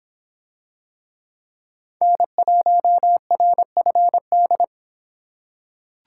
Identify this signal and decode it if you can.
Morse code, heard as a keyed tone, about 25 wpm, 700 Hz. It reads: N1RFD